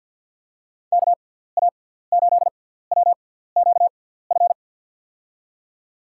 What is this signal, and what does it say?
Morse code, heard as a keyed tone, about 53 wpm, spaced as for 17 wpm, 700 Hz. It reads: KA8WQF
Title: USA Call Signs - 6 Characters - 6x Character Spacing - 50wpm